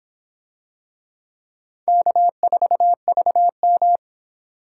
Morse code, heard as a keyed tone, about 24 wpm, 700 Hz. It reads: K4VM